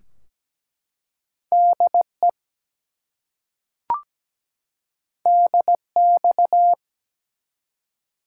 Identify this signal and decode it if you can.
Morse code, heard as a keyed tone, about 17 wpm, 700 Hz. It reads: DE DX
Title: Top 180 QSO Elements 17wpm